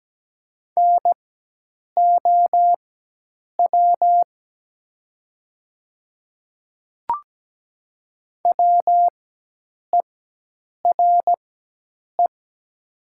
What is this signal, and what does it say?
Morse code, heard as a keyed tone, about 17 wpm, spaced as for 8 wpm, 700 Hz. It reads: NOW WERE
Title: 4x Character Spacing - Top 100 Words - 17wpm